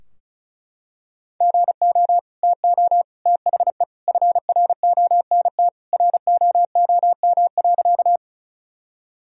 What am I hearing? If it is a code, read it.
Morse code, heard as a keyed tone, about 34 wpm, 700 Hz. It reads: GO TO THE FRONT ROOM.
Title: Sentences from Top 1000 Words - Review Part 2 - 35wpm